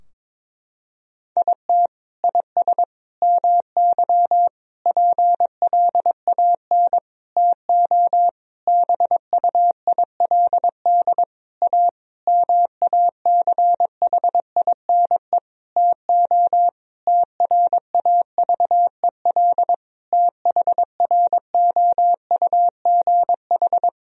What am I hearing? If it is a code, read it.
Morse code, heard as a keyed tone, about 22 wpm, 700 Hz. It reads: IT IS MY PLAN TO BUILD A MACHINE TO TRAVEL THROUGH
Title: Sentences from Top 1000 Words - Review Part 2 - 22wpm